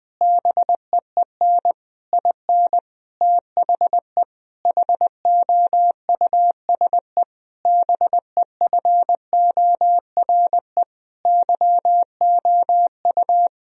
Morse code, heard as a keyed tone, about 19 wpm, 700 Hz. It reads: BEEN IN THE HOUSE BEFORE YOU
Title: Sentences from Top 1000 Words - Review Part 3 - 20wpm